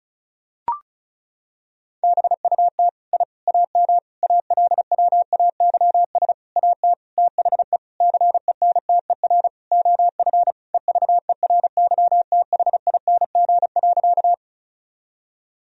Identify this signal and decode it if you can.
Morse code, heard as a keyed tone, about 36 wpm, 700 Hz. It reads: BUT I AM ALWAYS AT THE CENTER OF EVERYTHING.
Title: Sentences from Top 1000 Words - Review Part 2 - 35wpm